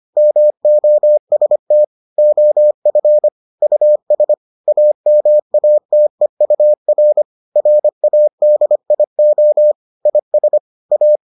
Morse code, heard as a keyed tone, about 25 wpm, 600 Hz. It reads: MOST OF US AMATEUR RADIO IS A